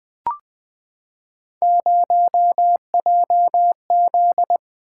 Morse code, heard as a keyed tone, about 20 wpm, 700 Hz. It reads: 0JZ
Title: Three Letter-Number Combinations - 20wpm